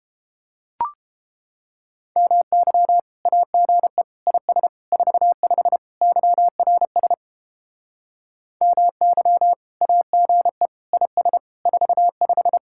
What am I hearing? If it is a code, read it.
Morse code, heard as a keyed tone, about 29 wpm, 700 Hz. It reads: MY AGE IS 45 YRS MY AGE IS 45